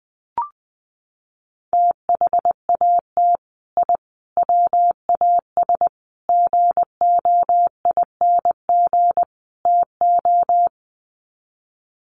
Morse code, heard as a keyed tone, about 20 wpm, 700 Hz. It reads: THAT I WAS GOING TO
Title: Sets of 5 Words Encore 20wpm